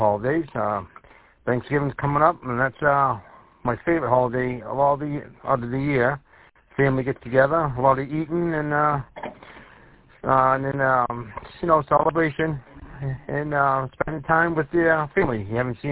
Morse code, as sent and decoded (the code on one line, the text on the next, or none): none